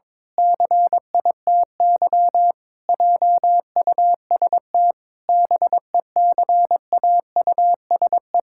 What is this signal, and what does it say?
Morse code, heard as a keyed tone, about 22 wpm, 700 Hz. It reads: CITY JUST BECAUSE
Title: Pangram Challenge v2 - 22wpm